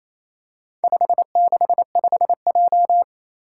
Morse code, heard as a keyed tone, about 28 wpm, 700 Hz. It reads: 565J
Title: Four Challenging Letter-Number Combinations 28wpm